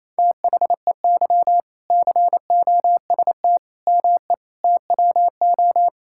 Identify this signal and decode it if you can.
Morse code, heard as a keyed tone, about 28 wpm, 700 Hz. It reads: THEY COST ME TWO
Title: Sentences from Top 1000 Words - Review Part 5 - 28wpm